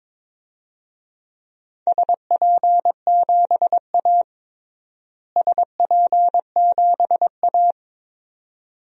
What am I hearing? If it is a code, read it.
Morse code, heard as a keyed tone, about 22 wpm, 700 Hz. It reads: SP7A SP7A
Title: DXCC-100 Call Signs - 4 Characters - Repeated 2x - 22wpm